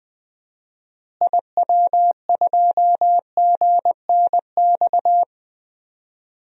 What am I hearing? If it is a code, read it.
Morse code, heard as a keyed tone, about 21 wpm, 700 Hz. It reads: IW2GNX
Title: DXCC-100 Call Signs - 6 Characters - Repeated 3x - 20wpm